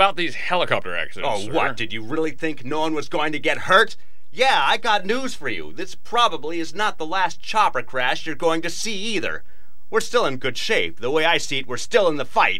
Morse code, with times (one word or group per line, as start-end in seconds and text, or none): none